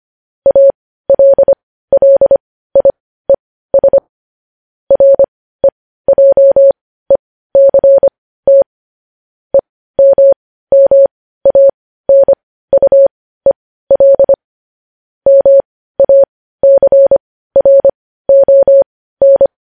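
Morse code, heard as a keyed tone, about 25 wpm, spaced as for 15 wpm, 550 Hz. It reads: ALLIES REJECT EMMANUEL MACRON
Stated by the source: Morse news headlines sent at 15 wpm.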